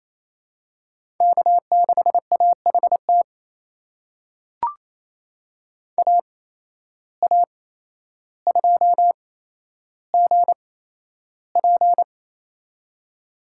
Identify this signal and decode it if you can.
Morse code, heard as a keyed tone, about 28 wpm, spaced as for 12 wpm, 700 Hz. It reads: K6AHT AA2GP
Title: USA Call Signs - 5 Characters - 8x Character Spacing - 28wpm